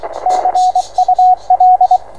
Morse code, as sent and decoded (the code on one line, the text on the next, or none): -.-. ..- .-..
CUL